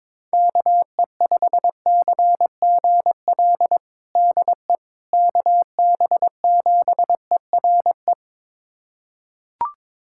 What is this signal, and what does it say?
Morse code, heard as a keyed tone, about 21 wpm, 700 Hz. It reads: KE5CGL DE KB7ERE
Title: USA Callsign DE Callsign - 6 Characters - 22wpm